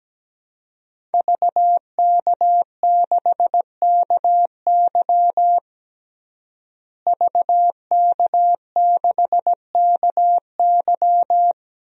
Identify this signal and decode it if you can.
Morse code, heard as a keyed tone, about 17 wpm, 700 Hz. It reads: VK6KY VK6KY
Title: DXCC-100 Call Signs - 5 Characters - Repeated 2x - 17wpm